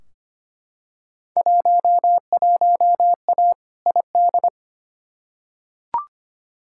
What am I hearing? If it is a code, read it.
Morse code, heard as a keyed tone, about 25 wpm, 700 Hz. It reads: 11A ID
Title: Field Day Exchanges Encore - 25wpm